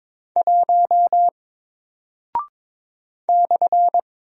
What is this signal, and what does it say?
Morse code, heard as a keyed tone, about 22 wpm, 700 Hz. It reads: 1 /
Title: Single Character (Letter:Number:Punctuation:Prosign) Alt 22wpm